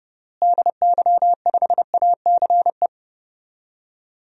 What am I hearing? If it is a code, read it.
Morse code, heard as a keyed tone, about 31 wpm, 700 Hz. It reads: DY5ACE
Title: DXCC-100 Call Signs - 6 Characters - Repeated 2x - 30wpm